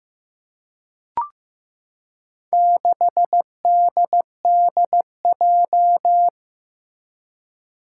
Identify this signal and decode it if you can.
Morse code, heard as a keyed tone, about 15 wpm, 700 Hz. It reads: 6DDJ
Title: Four Challenging Letter-Number Combinations 15wpm